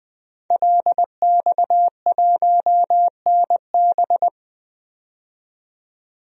Morse code, heard as a keyed tone, about 21 wpm, 700 Hz. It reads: LX1NB
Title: DXCC-100 Call Signs - 5 Characters - 20wpm